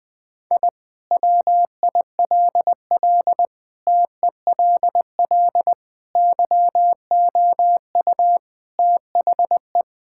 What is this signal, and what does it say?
Morse code, heard as a keyed tone, about 21 wpm, 700 Hz. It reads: I WILL TELL YOU THE
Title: Sentences with 4-Word Phrases - 20wpm